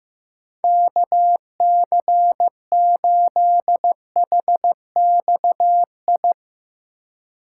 Morse code, heard as a keyed tone, about 15 wpm, 700 Hz. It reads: KC8HXI
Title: USA Call Signs - 3x Character Spacing - 15wpm